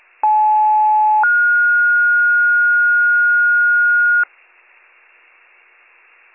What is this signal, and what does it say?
Two-tone page, heard as a one-second tone, then a three-second tone, about 860 then 1430 Hz. It is heard from 0.2 to 4.2 s.